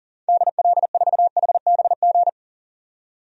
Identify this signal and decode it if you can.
Morse code, heard as a keyed tone, about 40 wpm, 700 Hz. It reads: DL4HBG
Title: DXCC-100 Call Signs - 6 Characters - 40wpm